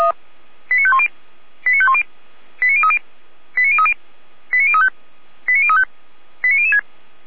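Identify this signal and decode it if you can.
Five-tone selective call (ZVEI), heard as quick runs of five tones, tones of about 70 ms. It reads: 89510 89510 89030 89030 89035 89035 89006
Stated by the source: Five-tone ZVEI selective call 89014, received by SDR.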